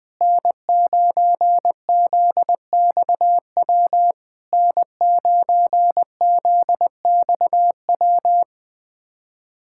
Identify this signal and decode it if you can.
Morse code, heard as a keyed tone, about 21 wpm, 700 Hz. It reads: N9ZXW N9ZXW
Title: SOTA QSO Practice 20wpm